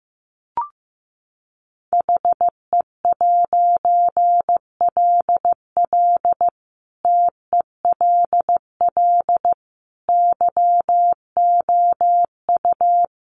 Morse code, heard as a keyed tone, about 15 wpm, 700 Hz. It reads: HE'LL TELL YOU
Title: A Sound Of Thunder 15wpm